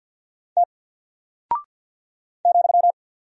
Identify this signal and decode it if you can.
Morse code, heard as a keyed tone, about 48 wpm, 700 Hz. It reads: T ,
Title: Single Character (Letter:Number:Punctuation:Prosign) Alt 50wpm